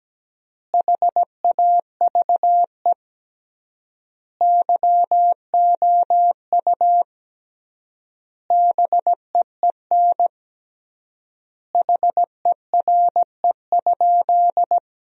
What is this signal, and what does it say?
Morse code, heard as a keyed tone, about 17 wpm, 700 Hz. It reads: HAVE YOU BEEN HERE?
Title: 3x Word-Spacing - Sentences from Top 200 Words - 17wpm